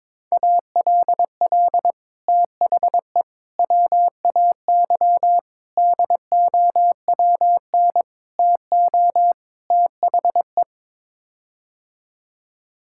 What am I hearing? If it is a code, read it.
Morse code, heard as a keyed tone, about 22 wpm, 700 Hz. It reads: ALL THE WAY DOWN TO THE
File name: practice / 22wpm / Sets of 6 Words - 22wpm